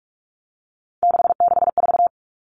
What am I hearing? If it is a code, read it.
Morse code, heard as a keyed tone, about 44 wpm, 700 Hz. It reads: 664
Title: Common Cut Numbers - 3-Digit 45wpm